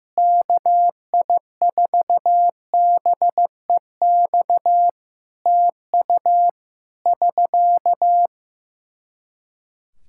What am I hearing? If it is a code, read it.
Morse code, heard as a keyed tone, about 15 wpm, 700 Hz. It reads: KI4BEX TU <SK>
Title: K1USN SST QSO Practice 15wpm